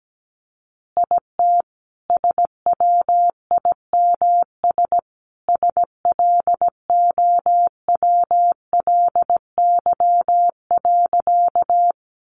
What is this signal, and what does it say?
Morse code, heard as a keyed tone, about 17 wpm, 700 Hz. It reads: IT SWIMS SLOWLY.
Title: Easy Phrases 17wpm